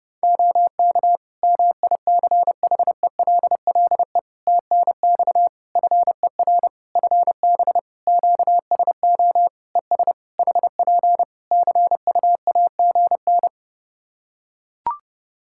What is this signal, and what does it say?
Morse code, heard as a keyed tone, about 30 wpm, 700 Hz. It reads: OK MICHELLE TNX FER FB QSO ES HP CUAGN
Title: Practice QSO - Exchange 4 - Ending - 30wpm